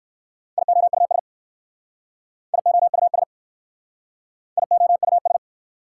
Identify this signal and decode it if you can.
Morse code, heard as a keyed tone, about 59 wpm, 690 Hz. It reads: IOUS IOUS IOUS